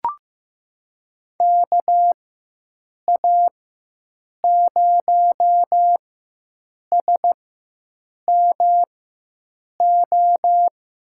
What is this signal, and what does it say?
Morse code, heard as a keyed tone, about 15 wpm, spaced as for 7 wpm, 700 Hz. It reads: KA0SMO
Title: USA Call Signs - 4x Character Spacing - 15wpm